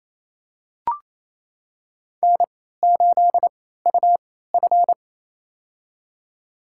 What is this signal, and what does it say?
Morse code, heard as a keyed tone, about 28 wpm, spaced as for 16 wpm, 700 Hz. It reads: N8UF
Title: USA Call Signs - 4 Characters - 3x Character Spacing - 28wpm